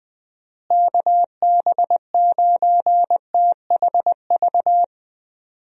Morse code, heard as a keyed tone, about 20 wpm, 700 Hz. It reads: KB9THV
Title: USA Call Signs - 6 Characters - Repeated 2x - 20wpm